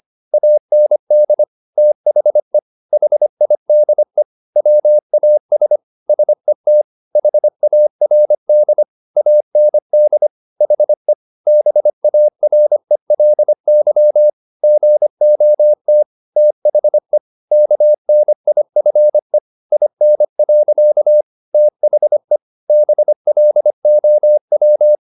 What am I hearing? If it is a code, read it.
Morse code, heard as a keyed tone, about 25 wpm, 600 Hz. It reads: AND THE HIDE WAS SET HARD AND HE BARELY GOT THE KNIFE IN. THE BLOW